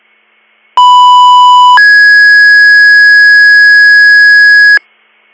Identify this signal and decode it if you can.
Two-tone page, heard as a one-second tone, then a three-second tone, about 990 then 1650 Hz. It is heard from 0.8 to 4.8 s.